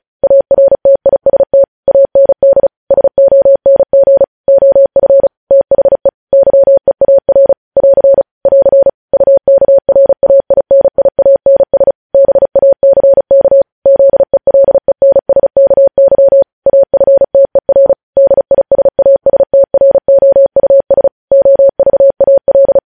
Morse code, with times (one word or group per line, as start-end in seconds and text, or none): 0.2-1.6 ARTIST
1.9-2.7 AND
2.9-4.2 SONG
4.5-5.3 OF
5.5-6.1 THE
6.3-7.5 YEAR
7.8-8.2 <AR>
8.5-8.9 <AR>
9.1-11.9 UKRAINIANS
12.1-13.6 BACK
13.9-16.4 ZELENSKY
16.7-17.9 AFTER
18.2-21.1 DISASTROUS
21.3-22.8 OVAL